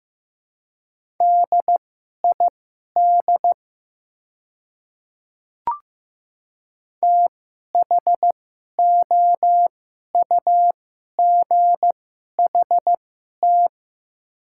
Morse code, heard as a keyed tone, about 15 wpm, spaced as for 11 wpm, 700 Hz. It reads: DID THOUGHT
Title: 2x Character Spacing - Top 300 Words - 15wpm